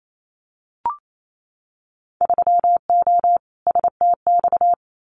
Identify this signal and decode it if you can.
Morse code, heard as a keyed tone, about 28 wpm, 700 Hz. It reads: 3O STX
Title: Winter Field Day Exchanges 28wpm